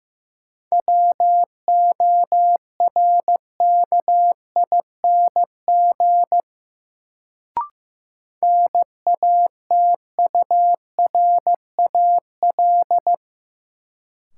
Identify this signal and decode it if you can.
Morse code, heard as a keyed tone, about 15 wpm, 700 Hz. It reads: WORKING NATURAL
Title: Sentences from Top 700 Words - New Word Drill - 15wpm